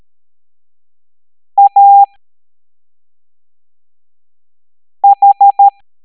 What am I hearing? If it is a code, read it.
Morse code, heard as a keyed tone, about 13 wpm, spaced as for 3 wpm, 800 Hz. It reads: AH